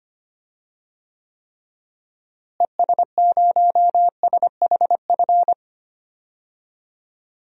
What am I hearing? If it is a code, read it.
Morse code, heard as a keyed tone, about 25 wpm, 700 Hz. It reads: ES0SHF
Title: DXCC-100 Call Signs - 6 Characters - Repeated 3x - 25wpm